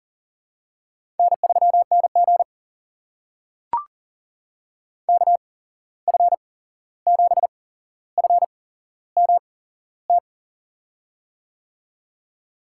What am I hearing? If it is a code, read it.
Morse code, heard as a keyed tone, about 39 wpm, spaced as for 15 wpm, 700 Hz. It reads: N3NG KF7FMT
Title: USA Call Signs - 8x Character Spacing - 40wpm